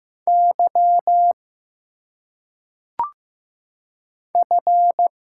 Morse code, heard as a keyed tone, about 15 wpm, 700 Hz. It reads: Y F
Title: Single Letters - Phonetic Pronunciation 15wpm